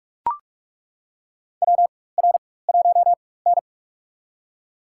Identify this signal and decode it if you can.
Morse code, heard as a keyed tone, about 43 wpm, spaced as for 21 wpm, 700 Hz. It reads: WR1N
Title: USA Call Signs - 4 Characters - 4x Character Spacing - 45wpm